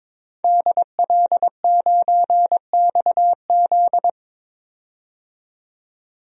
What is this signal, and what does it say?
Morse code, heard as a keyed tone, about 22 wpm, 700 Hz. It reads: DL9XZ
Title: DXCC-100 Call Signs - 5 Characters - 22wpm